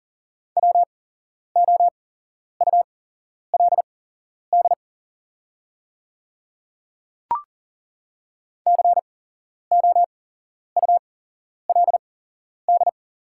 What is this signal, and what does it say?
Morse code, heard as a keyed tone, about 40 wpm, spaced as for 11 wpm, 690 Hz. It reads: WOULD COULD
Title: 8x Character Spacing - Top 100 Words - 40wpm